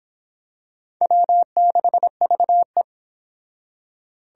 Morse code, heard as a keyed tone, about 25 wpm, 700 Hz. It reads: W6VE